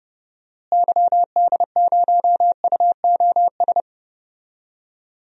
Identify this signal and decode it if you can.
Morse code, heard as a keyed tone, about 30 wpm, 700 Hz. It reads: YD0UOS
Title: DXCC-100 Call Signs - 6 Characters - Repeated 2x - 30wpm